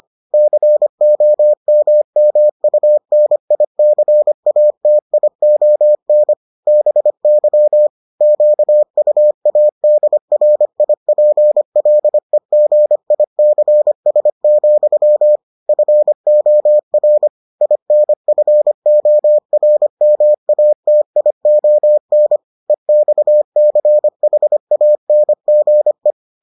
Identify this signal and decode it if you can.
Morse code, heard as a keyed tone, about 25 wpm, 600 Hz. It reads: COMMUNICATION BY QUADRIPLEGICS, FOR INFORMATION EXCHANGE